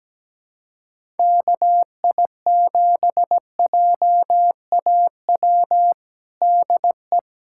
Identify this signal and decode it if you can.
Morse code, heard as a keyed tone, about 17 wpm, 700 Hz. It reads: KI7JAW DE
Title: USA Callsign DE Callsign 17wpm